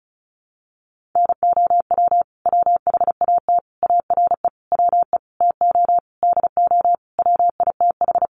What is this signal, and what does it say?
Morse code, heard as a keyed tone, about 34 wpm, 700 Hz. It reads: NOW WHAT ARE WE TO DO WITH
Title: Sentences from Top 100 Words 35wpm